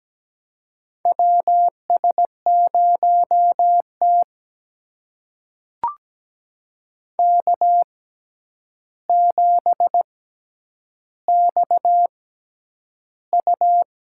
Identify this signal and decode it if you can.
Morse code, heard as a keyed tone, about 17 wpm, spaced as for 9 wpm, 700 Hz. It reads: WS0T K7XU
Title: USA Call Signs - 4 Characters - 6x Character Spacing - 17wpm